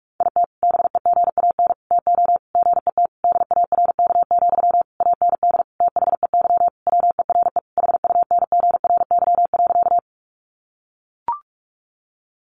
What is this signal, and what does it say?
Morse code, heard as a keyed tone, about 44 wpm, 700 Hz. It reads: IT BEGAN TO GET DARK, AND THEY WERE HUNGRY.
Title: Sentences from Top 1400 Words 45wpm